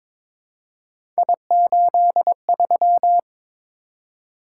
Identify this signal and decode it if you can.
Morse code, heard as a keyed tone, about 22 wpm, 700 Hz. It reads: I83
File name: practice / 22wpm / Three Letter-Number Combinations - 22wpm